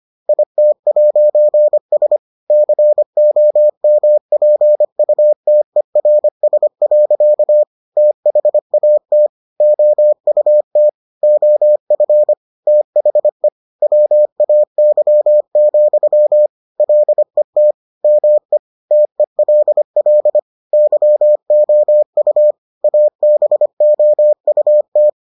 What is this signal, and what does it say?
Morse code, heard as a keyed tone, about 25 wpm, 600 Hz. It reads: IT'S COMPUTERS. THAT OUT OF THE WAY, LET ME TELL YOU ABOUT